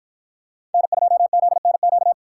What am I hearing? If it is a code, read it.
Morse code, heard as a keyed tone, about 50 wpm, 700 Hz. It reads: N2ZNQ